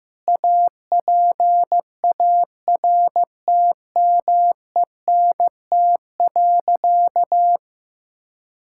Morse code, heard as a keyed tone, about 15 wpm, 700 Hz. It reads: APARTMENT.
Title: Sentences from Top 1000 Words - Review Part 6 - 15wpm